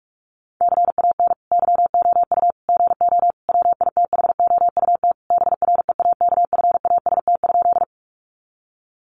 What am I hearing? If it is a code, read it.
Morse code, heard as a keyed tone, about 44 wpm, 700 Hz. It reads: CAN YOU GO WITHOUT BREAKFAST?
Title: Sentences from Top 1500 Words 45wpm